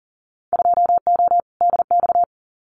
Morse code, heard as a keyed tone, about 40 wpm, 700 Hz. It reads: 2O DX
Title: Winter Field Day Exchanges Encore 40wpm